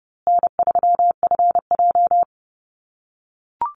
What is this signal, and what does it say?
Morse code, heard as a keyed tone, about 30 wpm, 700 Hz. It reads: N3FJ
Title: USA Call Signs Repeated 3x 30wpm